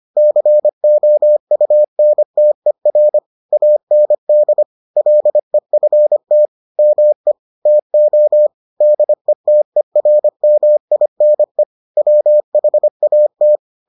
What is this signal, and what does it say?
Morse code, heard as a keyed tone, about 25 wpm, 600 Hz. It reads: COUNTER AND LEFT ME TO DETERMINE WHAT